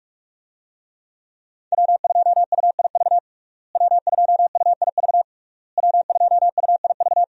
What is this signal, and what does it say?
Morse code, heard as a keyed tone, about 42 wpm, 700 Hz. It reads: W2UIV W2UIV W2UIV
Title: USA Call Signs - 5 Characters - Repeated 3x - 45wpm